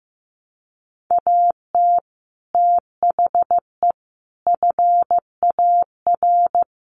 Morse code, heard as a keyed tone, about 15 wpm, 700 Hz. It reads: AT THE FAR
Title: Sets of 5 Words 15wpm